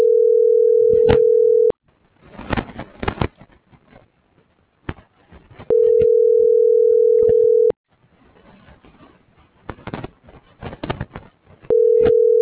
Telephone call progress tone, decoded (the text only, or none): ringback tone